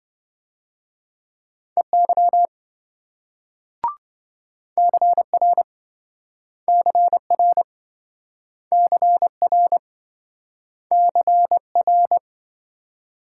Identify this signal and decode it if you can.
Morse code, heard as a keyed tone, about 27 wpm, 700 Hz. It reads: EY CR CR CR CR